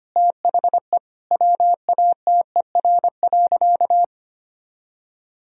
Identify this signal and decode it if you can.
Morse code, heard as a keyed tone, about 25 wpm, 700 Hz. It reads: THE WATER.
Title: Sentences from Top 1000 Words - Review Part 1 - 25wpm